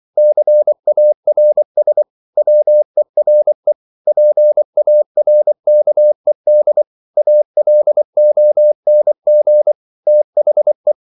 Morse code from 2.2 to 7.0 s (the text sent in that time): WERE PARKED